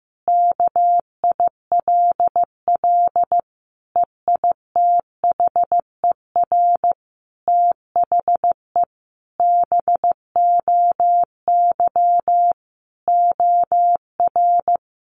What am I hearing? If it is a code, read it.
Morse code, heard as a keyed tone, about 15 wpm, 700 Hz. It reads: KILL EITHER THE BOY OR
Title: Sentences from Top 1500 Words 15wpm